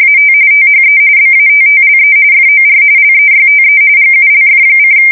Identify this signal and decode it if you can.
RTTY, about 45.45 baud. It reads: AT THE BATTLE FOR LEYTE GULF:
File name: fsk170 / battleshipnj